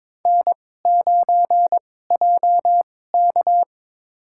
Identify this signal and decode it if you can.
Morse code, heard as a keyed tone, about 22 wpm, spaced as for 16 wpm, 700 Hz. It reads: N9JK
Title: USA Call Signs - 4 Characters - 2x Character Spacing - 22wpm